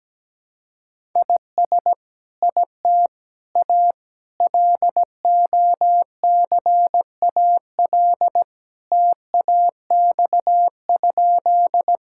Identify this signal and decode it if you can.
Morse code, heard as a keyed tone, about 17 wpm, 700 Hz. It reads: IS IT A LOCAL TAX?
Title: Sentences from Top 1000 Words - Review Part 5 - 17wpm